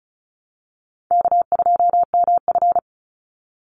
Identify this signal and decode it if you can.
Morse code, heard as a keyed tone, about 34 wpm, 700 Hz. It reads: K2MF
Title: Easy Call Signs Repeated 2x - 35wpm